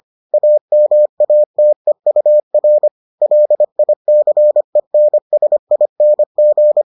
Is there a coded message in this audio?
Yes: Morse code from 0.3 to 6.8 s, heard as a keyed tone, about 25 wpm, 600 Hz. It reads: AMATEUR LICENSING